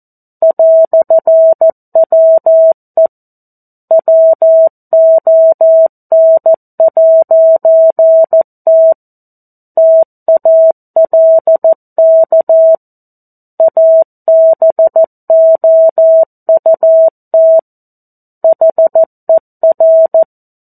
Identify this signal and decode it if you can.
Morse code, heard as a keyed tone, about 14 wpm, 650 Hz. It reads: "WE WON'T TALK ABOUT HER
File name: 02-Chapter-02 14 WPM